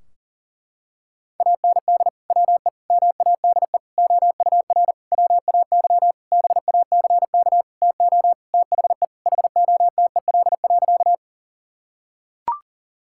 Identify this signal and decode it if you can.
Morse code, heard as a keyed tone, about 37 wpm, 700 Hz. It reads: AND WE MADE OUR WAY BACK TO THE HOTEL.
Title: Sentences from Top 1600 Words 40wpm